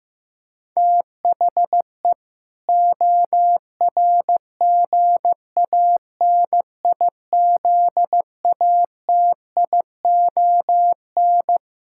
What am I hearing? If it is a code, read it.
Morse code, heard as a keyed tone, about 15 wpm, 700 Hz. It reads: THE ORGANIZATION
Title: Sentences from Top 1000 Words - Review Part 6 - 15wpm